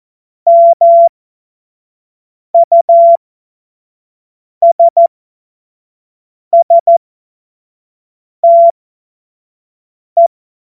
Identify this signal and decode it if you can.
Morse code, heard as a keyed tone, about 14 wpm, spaced as for 5 wpm, 680 Hz. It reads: MUSSTE
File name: Morsekurs / Witze / Witz 27. Dezember 2022 25 F70